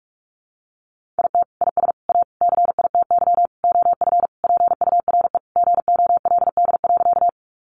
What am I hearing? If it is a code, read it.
Morse code, heard as a keyed tone, about 44 wpm, 700 Hz. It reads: IT IS A CITY OF PURE GOLD.
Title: Sentences from Top 1500 Words 45wpm